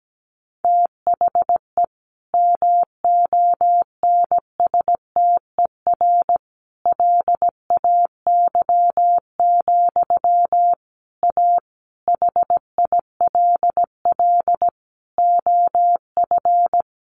Morse code, heard as a keyed tone, about 17 wpm, 700 Hz. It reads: THE MONSTER LAY, A HILL OF
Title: A Sound Of Thunder 17wpm